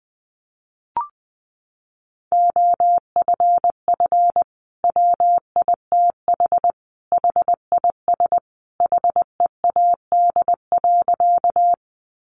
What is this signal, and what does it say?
Morse code, heard as a keyed tone, about 20 wpm, 700 Hz. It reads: OFF WITH HIS HEAD.
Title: Sentences from Top 300 Words 20wpm